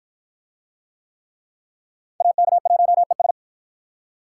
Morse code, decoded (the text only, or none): AK1ES